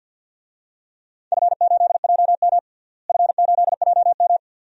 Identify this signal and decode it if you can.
Morse code, heard as a keyed tone, about 51 wpm, 700 Hz. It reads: F8JM F8JM
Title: DXCC-100 Call Signs - 4 Characters - Repeated 2x - 50wpm